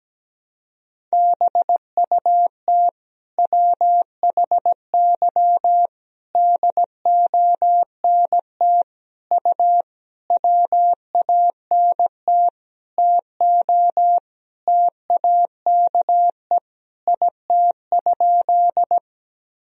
Morse code, heard as a keyed tone, about 17 wpm, 700 Hz. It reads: BUT WHY DONT U WANT TO TAKE IT?
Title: Sentences from Top 300 Words and Abbreviations 17wpm